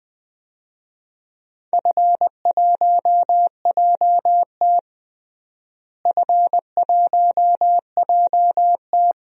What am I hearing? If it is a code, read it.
Morse code, heard as a keyed tone, about 20 wpm, 700 Hz. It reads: F1JT F1JT
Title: DXCC-100 Call Signs - 4 Characters - Repeated 3x - 20wpm